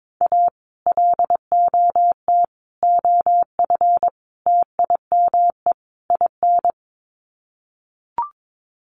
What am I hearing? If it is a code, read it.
Morse code, heard as a keyed tone, about 22 wpm, 700 Hz. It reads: A LOT OF TIME IN
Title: Sets of 5 Words Encore2 22wpm